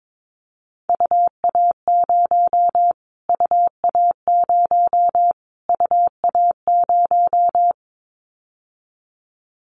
Morse code, heard as a keyed tone, about 22 wpm, 700 Hz. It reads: UA0 UA0 UA0
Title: Callsign Prefixes for Top 25 DXCC - Repeated 3x - 22wpm